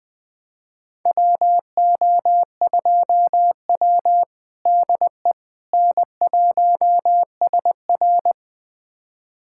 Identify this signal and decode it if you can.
Morse code, heard as a keyed tone, about 20 wpm, 700 Hz. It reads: WO2W DE N1SR
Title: USA Callsign DE Callsign - 4 Characters - 20wpm